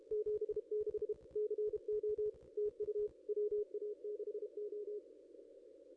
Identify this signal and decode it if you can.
Morse code, heard as a keyed tone, about 32 wpm, 410 Hz. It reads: 76 CO TU WA6O
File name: contests / ss-cw-k5tr-op-n5rz -1900